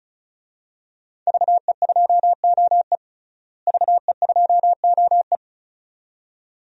Morse code, heard as a keyed tone, about 34 wpm, 700 Hz. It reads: VE2OE VE2OE
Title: DXCC-100 Call Signs - 5 Characters - Repeated 2x - 35wpm